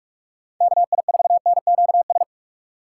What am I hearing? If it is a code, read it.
Morse code, heard as a keyed tone, about 43 wpm, 700 Hz. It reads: KI4NQS
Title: USA Call Signs - 2x Character Spacing - 45wpm